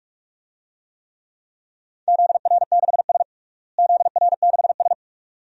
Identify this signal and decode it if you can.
Morse code, heard as a keyed tone, about 44 wpm, 700 Hz. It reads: ZR6S ZR6S
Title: DXCC-100 Call Signs - 4 Characters - Repeated 2x - 45wpm